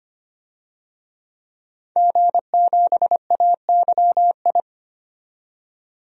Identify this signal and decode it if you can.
Morse code, heard as a keyed tone, about 25 wpm, 690 Hz. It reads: G7AYI
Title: DXCC-100 Call Signs - 5 Characters - 25wpm